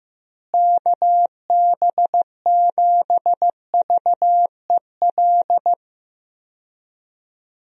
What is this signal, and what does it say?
Morse code, heard as a keyed tone, about 15 wpm, 700 Hz. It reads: KB7VEL
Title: USA Call Signs - 6 Characters - Repeated 3x - 15wpm